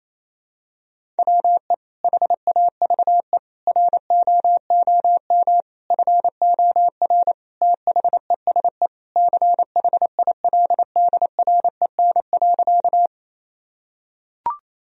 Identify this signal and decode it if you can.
Morse code, heard as a keyed tone, about 28 wpm, 700 Hz. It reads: WE HAVE ROOM FOR THESE CHILDREN.
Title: Sentences from Top 1000 Words - Review Part 1 - 28wpm